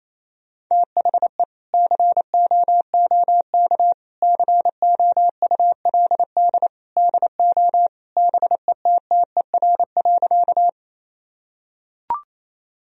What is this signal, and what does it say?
Morse code, heard as a keyed tone, about 28 wpm, 700 Hz. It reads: THE COOK COULD DO BETTER.
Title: Sentences from Top 1000 Words - Review Part 4 - 28wpm